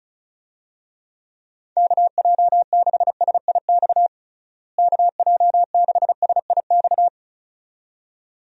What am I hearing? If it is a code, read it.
Morse code, heard as a keyed tone, about 34 wpm, 700 Hz. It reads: KJ6SIX KJ6SIX
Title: USA Call Signs - 6 Characters - Repeated 2x - 35wpm